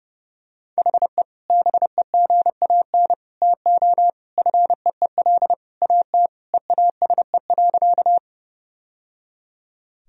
Morse code, heard as a keyed tone, about 30 wpm, 700 Hz. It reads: HE BEGAN TO FEEL AT EASE.
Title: Sentences from Top 1000 Words - Review Part 1 - 30wpm